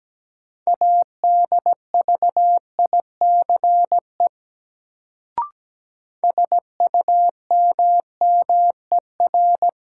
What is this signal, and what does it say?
Morse code, heard as a keyed tone, about 17 wpm, 700 Hz. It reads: ADVICE SUMMER
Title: Sentences from Top 1100 Words - New Word Drill Alt - 17wpm